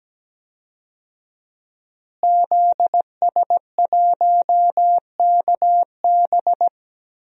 Morse code, heard as a keyed tone, about 17 wpm, 700 Hz. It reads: ZS1KB
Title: DXCC-100 Call Signs - 5 Characters - Repeated 2x - 17wpm